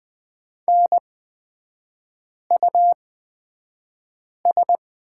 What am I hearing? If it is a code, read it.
Morse code, heard as a keyed tone, about 20 wpm, spaced as for 5 wpm, 700 Hz. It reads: NUS
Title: 224 Lesson 36e - BT - Three Characters 20wpm